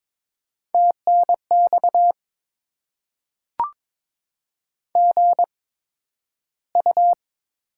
Morse code, heard as a keyed tone, about 22 wpm, spaced as for 12 wpm, 700 Hz. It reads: TNX GU
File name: Top 20 QSO Elements - 8x Character Spacing - 22wpm